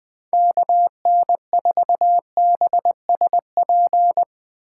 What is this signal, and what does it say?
Morse code, heard as a keyed tone, about 20 wpm, 700 Hz. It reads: KN4BSP
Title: Speed Racing - USA 2x3 Call Signs 17wpm